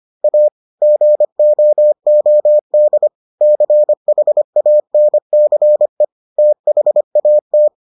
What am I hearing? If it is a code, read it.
Morse code, heard as a keyed tone, about 25 wpm, 600 Hz. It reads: A GOOD CHANCE THAT